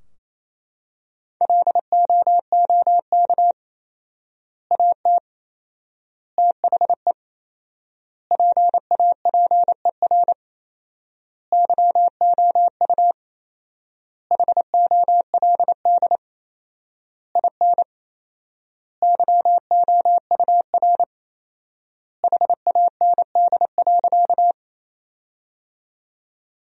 Morse code, heard as a keyed tone, about 27 wpm, 700 Hz. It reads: LOOK AT THE PAPER YOU HOLD IN YOUR HAND.
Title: 4x Word-Spacing - Sentences from Top 400 Words - 28wpm